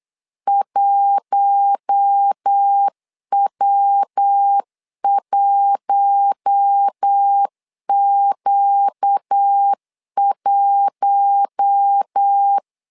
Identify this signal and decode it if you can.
Morse code, heard as a keyed tone, about 8 wpm, 800 Hz. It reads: 1W1Q1
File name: LEZIONI / 4 Q1 8wpm 3035